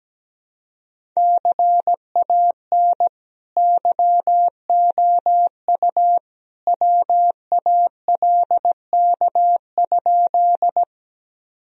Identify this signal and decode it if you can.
Morse code, heard as a keyed tone, about 17 wpm, 700 Hz. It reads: CAN YOU WALK?
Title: Sentences from Top 1000 Words - Review Part 1 - 17wpm